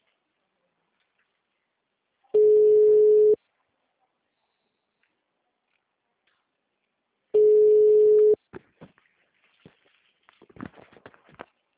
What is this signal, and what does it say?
Ringback tone (Europe, 425 Hz).